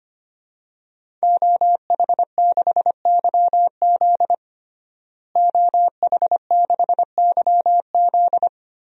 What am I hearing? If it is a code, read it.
Morse code, heard as a keyed tone, about 25 wpm, 700 Hz. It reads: OH6YZ OH6YZ
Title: DXCC-100 Call Signs - 5 Characters - Repeated 2x - 25wpm